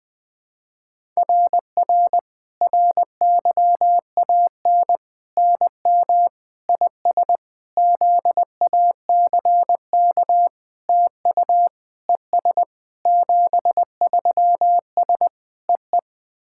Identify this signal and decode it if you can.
Morse code, heard as a keyed tone, about 20 wpm, 700 Hz. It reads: RR RYAN NM IS ZACK TU ES 73S EE
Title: SOTA QSO Practice 20wpm